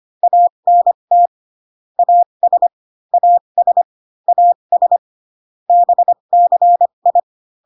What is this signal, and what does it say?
Morse code, heard as a keyed tone, about 25 wpm, spaced as for 22 wpm, 700 Hz. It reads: ANT AS AS AS BCI
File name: CW common abbreviations alphabetical repeated 3 times 25 18wpm